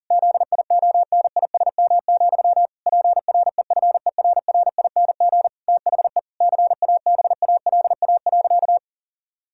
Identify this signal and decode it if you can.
Morse code, heard as a keyed tone, about 40 wpm, 700 Hz. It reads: ZIONISM, PREFERRING THE CABALA.
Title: Pangram Challenge v2 - 40wpm